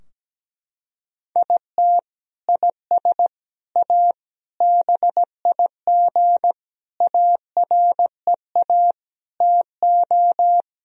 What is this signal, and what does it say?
Morse code, heard as a keyed tone, about 17 wpm, 700 Hz. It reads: IT IS A BIG AREA TO
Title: Sentences from Top 1000 Words - Review Part 2 - 17wpm